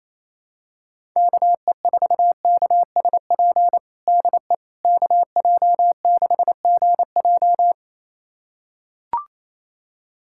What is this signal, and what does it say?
Morse code, heard as a keyed tone, about 28 wpm, 700 Hz. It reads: KE4KSP DE KJ6GJ
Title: USA Callsign DE Callsign 28wpm